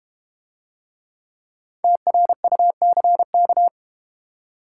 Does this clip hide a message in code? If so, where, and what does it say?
Yes: Morse code from 1.8 to 3.7 s, heard as a keyed tone, about 32 wpm, 700 Hz. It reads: TRUCK